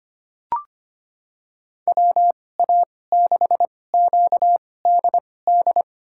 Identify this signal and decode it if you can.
Morse code, heard as a keyed tone, about 25 wpm, spaced as for 18 wpm, 700 Hz. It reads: WA6QDD
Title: USA Call Signs - 6 Characters - 2x Character Spacing - 25wpm